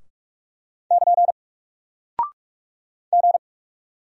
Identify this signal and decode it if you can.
Morse code, heard as a keyed tone, about 44 wpm, 700 Hz. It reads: <KN> G
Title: Single Character (Letter:Number:Punctuation:Prosign) Alt 45wpm